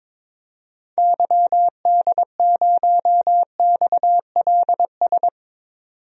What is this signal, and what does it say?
Morse code, heard as a keyed tone, about 22 wpm, 700 Hz. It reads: YD0XLS
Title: DXCC-100 Call Signs - 6 Characters - 22wpm